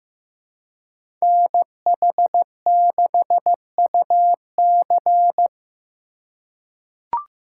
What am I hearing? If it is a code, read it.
Morse code, heard as a keyed tone, about 15 wpm, 700 Hz. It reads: NH6UC
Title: USA Call Signs - 4x Character Spacing - 15wpm